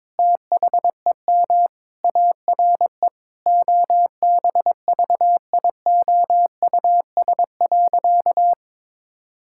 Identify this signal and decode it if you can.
Morse code, heard as a keyed tone, about 22 wpm, 700 Hz. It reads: THEM ARE OBVIOUS.
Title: Sentences from Top 1600 Words 22wpm